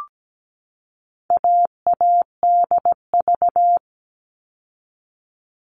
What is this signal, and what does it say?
Morse code, heard as a keyed tone, about 17 wpm, 700 Hz. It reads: AADV
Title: All Cut Numbers - 4-Digit 17wpm